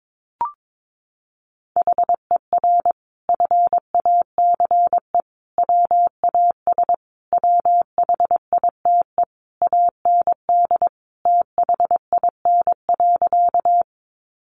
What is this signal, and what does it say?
Morse code, heard as a keyed tone, about 22 wpm, 700 Hz. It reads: HER FACE WAS WHITE AND THIN.
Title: Sentences from Top 1200 Words 22wpm